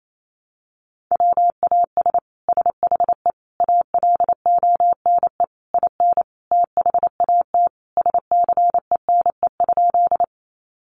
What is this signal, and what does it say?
Morse code, heard as a keyed tone, about 28 wpm, 700 Hz. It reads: WAS SHE ALONE IN THAT SCENE?
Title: Sentences from Top 1100 Words 28wpm